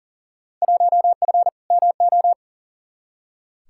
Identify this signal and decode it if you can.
Morse code, heard as a keyed tone, about 40 wpm, 700 Hz. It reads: 1F MO
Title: Field Day Exchanges Encore - 40wpm